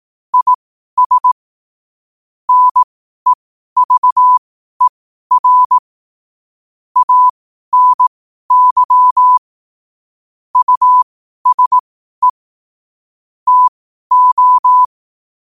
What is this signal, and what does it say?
Morse code, heard as a keyed tone, about 18 wpm, spaced as for 12 wpm, 1000 Hz. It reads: IS NEVER ANY USE TO